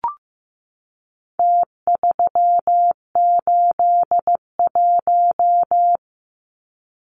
Morse code, heard as a keyed tone, about 15 wpm, 700 Hz. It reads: T381
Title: Numbers - Contest Style - 4-Digit 15wpm